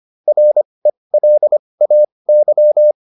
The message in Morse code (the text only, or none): RELAY